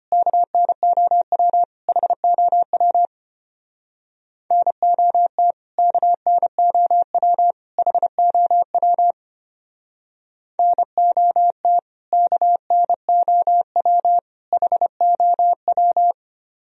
Morse code, heard as a keyed tone, about 31 wpm, 700 Hz. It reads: KNOW HOW NOT KNOW HOW NOT KNOW HOW
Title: Speed-Racing - Sets of 3 Words 25wpm